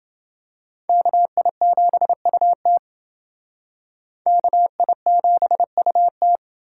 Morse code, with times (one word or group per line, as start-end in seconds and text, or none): 0.9-2.8 KI7UT
4.3-6.4 KI7UT